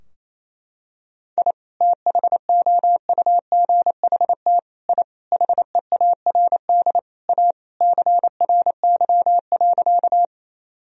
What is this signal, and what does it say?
Morse code, heard as a keyed tone, about 28 wpm, 700 Hz. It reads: I THOUGHT I HEARD A CRY.
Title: Sentences from Top 1000 Words - Review Part 2 - 28wpm